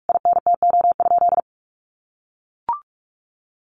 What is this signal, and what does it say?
Morse code, heard as a keyed tone, about 44 wpm, 700 Hz. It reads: INTO?
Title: Sentences from Top 600 Words 45wpm